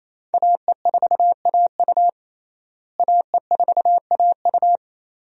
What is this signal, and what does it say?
Morse code, heard as a keyed tone, about 27 wpm, 700 Hz. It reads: AE4AU AE4AU
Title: USA Call Signs - 5 Characters - Repeated 2x - 28wpm